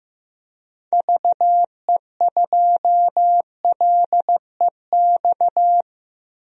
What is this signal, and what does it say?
Morse code, heard as a keyed tone, about 15 wpm, 700 Hz. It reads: VE2LEX
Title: DXCC-100 Call Signs - 6 Characters - Repeated 2x - 15wpm